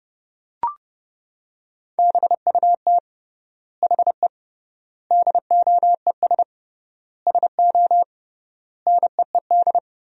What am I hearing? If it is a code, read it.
Morse code, heard as a keyed tone, about 31 wpm, 700 Hz. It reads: BUT HE DOES SO NEED
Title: 3x Word-Spacing - Sentences from Top 200 Words - 30wpm